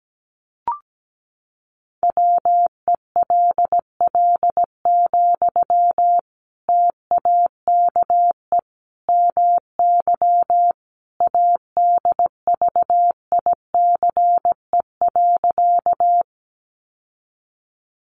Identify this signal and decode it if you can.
Morse code, heard as a keyed tone, about 17 wpm, 700 Hz. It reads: WELL, TAKE MY ADVICE.
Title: Sentences from Top 1100 Words 17wpm